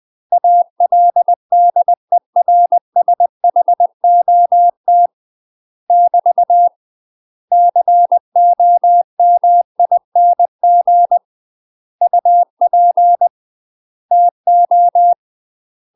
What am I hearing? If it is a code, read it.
Morse code, heard as a keyed tone, about 20 wpm, 700 Hz. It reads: ALDERSHOT <BT> COMING UP TO
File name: UK MASTER QSOs TRAINING / 20WPM / UK MASTER QSOs 0023